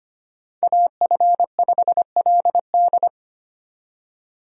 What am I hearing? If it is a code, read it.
Morse code, heard as a keyed tone, about 25 wpm, 700 Hz. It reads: AF5LD
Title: USA Call Signs - 5 Characters - 25wpm